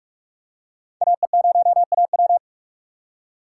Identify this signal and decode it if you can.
Morse code, heard as a keyed tone, about 44 wpm, 700 Hz. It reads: AE0AW
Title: USA Call Signs - 5 Characters - 2x Character Spacing - 45wpm